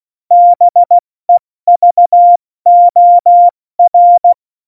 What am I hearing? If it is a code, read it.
Morse code, heard as a keyed tone, about 16 wpm, 700 Hz. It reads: BEVOR
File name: Morsekurs / Witze / Witz 22. Mai 2018, 70 F80